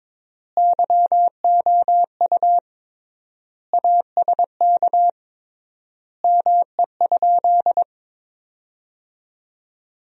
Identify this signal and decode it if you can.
Morse code, heard as a keyed tone, about 22 wpm, 700 Hz. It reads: YOU ASK ME?
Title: 3x Word-Spacing - Sentences from Top 200 Words - 22wpm